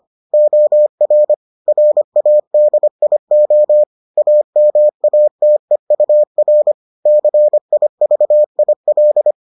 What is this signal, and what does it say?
Morse code, heard as a keyed tone, about 25 wpm, 600 Hz. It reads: OR RADIO AMATEUR CIVIL